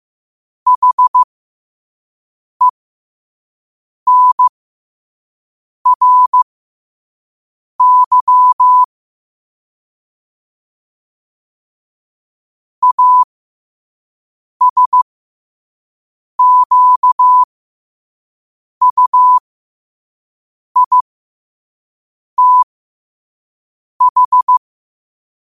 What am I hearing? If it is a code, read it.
Morse code, heard as a keyed tone, about 15 wpm, spaced as for 5 wpm, 1000 Hz. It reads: HENRY ASQUITH
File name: qotdfiles / 5wpm